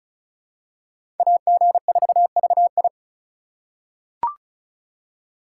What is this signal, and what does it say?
Morse code, heard as a keyed tone, about 34 wpm, 700 Hz. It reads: AG4VI
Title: USA Call Signs - 5 Characters - 4x Character Spacing - 35wpm